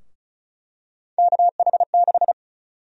Morse code, heard as a keyed tone, about 36 wpm, 700 Hz. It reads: KH6
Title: USA Call Sign Prefixes - 35wpm